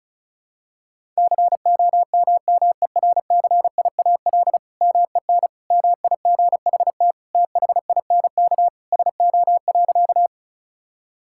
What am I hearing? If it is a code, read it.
Morse code, heard as a keyed tone, about 36 wpm, 700 Hz. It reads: COMMERCIAL MEN MIGHT THINK SO.
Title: Sentences from Top 1000 Words - Review Part 5 - 35wpm